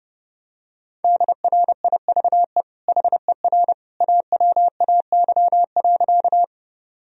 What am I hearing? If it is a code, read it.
Morse code, heard as a keyed tone, about 30 wpm, 700 Hz. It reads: DRIVE HER AWAY.
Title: Sentences from Top 1000 Words - Review Part 2 - 30wpm